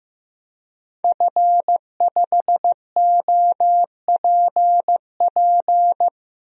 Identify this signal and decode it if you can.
Morse code, heard as a keyed tone, about 15 wpm, 700 Hz. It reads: F5OPP
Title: DXCC-100 Call Signs - 5 Characters - Repeated 2x - 15wpm